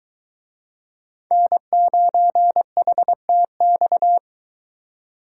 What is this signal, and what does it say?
Morse code, heard as a keyed tone, about 23 wpm, 700 Hz. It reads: N9HTX